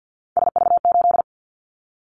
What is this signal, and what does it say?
Morse code, heard as a keyed tone, about 50 wpm, 700 Hz. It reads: SV8
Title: Callsign Prefixes for Top 50 DXCC - 50wpm